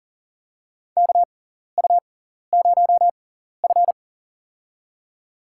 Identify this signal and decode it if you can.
Morse code, heard as a keyed tone, about 40 wpm, spaced as for 14 wpm, 700 Hz. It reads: KU0F
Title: USA Call Signs - 4 Characters - 6x Character Spacing - 40wpm